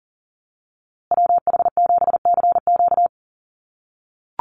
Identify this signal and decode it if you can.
Morse code, heard as a keyed tone, about 40 wpm, 700 Hz. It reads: WH7CQ
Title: USA Call Signs Repeated 2x 40wpm